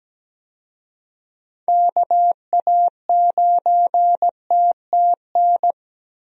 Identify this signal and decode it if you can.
Morse code, heard as a keyed tone, about 17 wpm, 700 Hz. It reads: KA9TTN
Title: USA Call Signs - 6 Characters - Repeated 2x - 17wpm